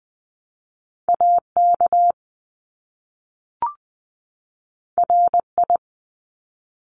Morse code, heard as a keyed tone, about 20 wpm, 700 Hz. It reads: AK RI
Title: ARRL Sections 20wpm